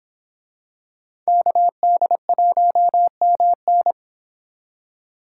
Morse code, heard as a keyed tone, about 26 wpm, 700 Hz. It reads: KD1MN